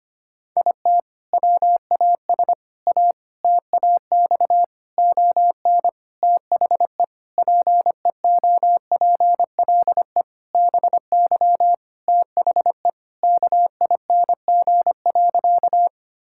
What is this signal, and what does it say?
Morse code, heard as a keyed tone, about 25 wpm, 700 Hz. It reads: IT WAS A TAX ON THE PEOPLE BY THE KING.
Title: Sentences from Top 1000 Words - Review Part 5 - 25wpm